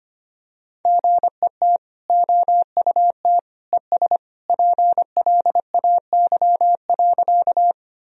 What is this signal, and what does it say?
Morse code, heard as a keyed tone, about 25 wpm, 700 Hz. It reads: GET OUT ES PLAY.
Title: Sentences from Top 200 Words and Abbreviations 25wpm